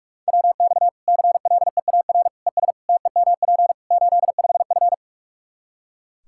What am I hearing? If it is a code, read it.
Morse code, heard as a keyed tone, about 45 wpm, 700 Hz. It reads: WX CLEAR ES TEMP 85F
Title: Practice QSO - Exchange 2 - Info - 45wpm